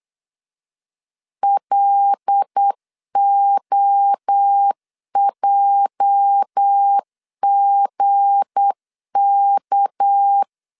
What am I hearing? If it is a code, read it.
Morse code, heard as a keyed tone, about 8 wpm, 800 Hz. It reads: LOJGK
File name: LEZIONI / 8 YJ45 / 8 YJ4 8wpm 3246